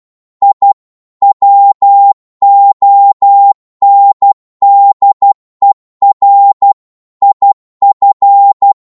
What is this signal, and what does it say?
Morse code, heard as a keyed tone, about 12 wpm, 800 Hz. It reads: I WONDER IF